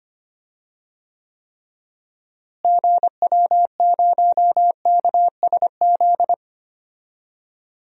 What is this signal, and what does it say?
Morse code, heard as a keyed tone, about 25 wpm, 700 Hz. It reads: GW0KSZ
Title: DXCC-100 Call Signs - 6 Characters - Repeated 3x - 25wpm